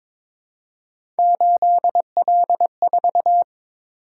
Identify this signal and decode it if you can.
Morse code, heard as a keyed tone, about 23 wpm, 700 Hz. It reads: 8L4
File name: practice / 22wpm / Three Letter-Number Combinations - 22wpm